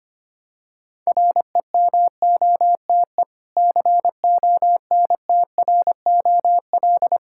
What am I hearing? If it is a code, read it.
Morse code, heard as a keyed tone, about 25 wpm, 700 Hz. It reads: REMOTE CONTROL
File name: practice / 25wpm / Difficult Words with Hints - 25wpm